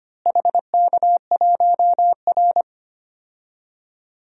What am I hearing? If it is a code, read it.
Morse code, heard as a keyed tone, about 25 wpm, 700 Hz. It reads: HK1R
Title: DXCC-100 Call Signs - 4 Characters - Repeated 3x - 25wpm